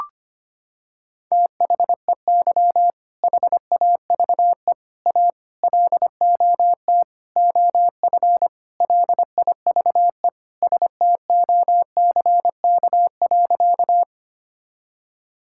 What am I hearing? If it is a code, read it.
Morse code, heard as a keyed tone, about 25 wpm, 700 Hz. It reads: THEY HAVE A LOT OF LIVE STOCK.
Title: Sentences from Top 1000 Words - Review Part 5 - 25wpm